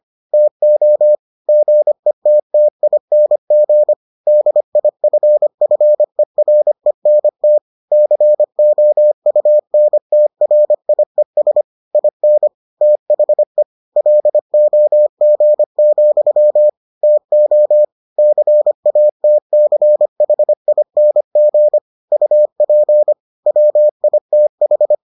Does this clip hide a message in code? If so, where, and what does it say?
Yes: Morse code from 0.3 to 24.9 s, heard as a keyed tone, about 25 wpm, 600 Hz. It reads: TO GETTING DIFFERENT COUNTRIES IN THE LOG, TO CATCHING UP WITH